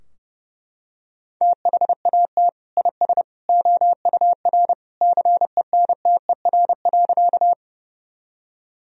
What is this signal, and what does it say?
Morse code, heard as a keyed tone, about 30 wpm, 700 Hz. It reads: THAT IS OUR CENTER.
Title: Sentences from Top 1000 Words - Review Part 2 - 30wpm